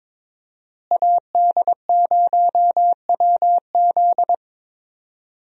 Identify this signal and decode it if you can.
Morse code, heard as a keyed tone, about 22 wpm, 700 Hz. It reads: AD0WZ
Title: USA Call Signs - 5 Characters - Repeated 2x - 22wpm